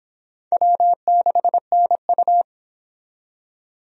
Morse code, heard as a keyed tone, about 25 wpm, 700 Hz. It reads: W6NU